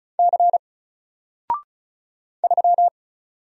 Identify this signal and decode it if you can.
Morse code, heard as a keyed tone, about 35 wpm, 700 Hz. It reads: C 3
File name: Single-Letter-Number-35wpm